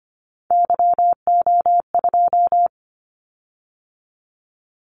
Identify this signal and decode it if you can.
Morse code, heard as a keyed tone, about 25 wpm, 700 Hz. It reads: YO2
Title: Callsign Prefixes for Top 25 DXCC - 25wpm